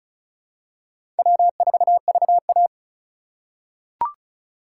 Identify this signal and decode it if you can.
Morse code, heard as a keyed tone, about 36 wpm, 700 Hz. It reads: W4VA
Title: USA Call Signs - 4 Characters - 3x Character Spacing - 35wpm